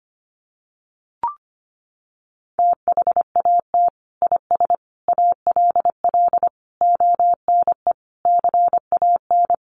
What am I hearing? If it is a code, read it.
Morse code, heard as a keyed tone, about 25 wpm, 700 Hz. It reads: THAT IS ALL ONE CAN
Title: Sentences from Top 200 Words 25wpm